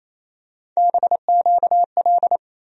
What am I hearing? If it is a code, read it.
Morse code, heard as a keyed tone, about 28 wpm, 700 Hz. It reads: BQL
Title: USA Call Sign Postfixes - 3 Letters 28wpm